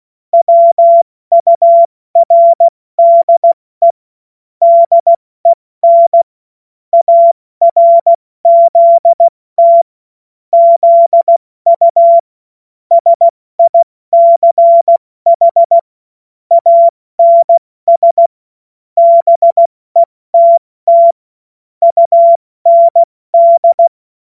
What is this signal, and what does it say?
Morse code, heard as a keyed tone, about 16 wpm, 680 Hz. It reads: WURDE DEN ARZT ZU SICH ANS BETT UND
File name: Morsekurs / Witze / Witz 22. Juni 2021, 70 F80